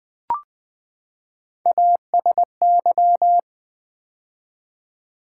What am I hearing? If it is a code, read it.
Morse code, heard as a keyed tone, about 20 wpm, 700 Hz. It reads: ASY
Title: Build A Word - 3 Letters At A Time - 6-Letter Words - 20wpm